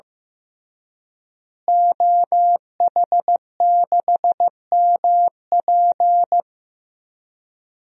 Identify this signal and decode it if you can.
Morse code, heard as a keyed tone, about 16 wpm, 700 Hz. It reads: OH6MP
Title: DXCC-100 Call Signs - 5 Characters - Repeated 3x - 15wpm